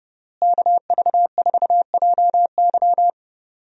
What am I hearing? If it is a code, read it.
Morse code, heard as a keyed tone, about 31 wpm, 700 Hz. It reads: KV4JY
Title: USA Call Signs - 5 Characters - Repeated 3x - 30wpm